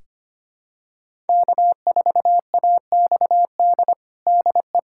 Morse code, heard as a keyed tone, about 25 wpm, 700 Hz. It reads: K4AXD DE
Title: USA Callsign DE Callsign 25wpm